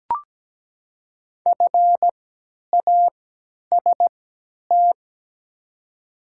Morse code, heard as a keyed tone, about 17 wpm, spaced as for 10 wpm, 700 Hz. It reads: FAST
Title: 3x Character Spacing - Top 500 Words - 17wpm